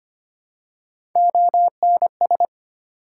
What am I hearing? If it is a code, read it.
Morse code, heard as a keyed tone, about 25 wpm, 700 Hz. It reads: ONS
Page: morse-code-ninja-practice-25wpm